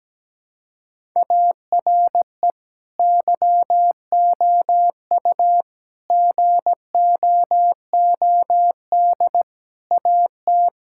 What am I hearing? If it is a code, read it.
Morse code, heard as a keyed tone, about 17 wpm, 700 Hz. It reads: ARE YOU GOOD AT
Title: Sentences from Top 1000 Words - Review Part 6 - 17wpm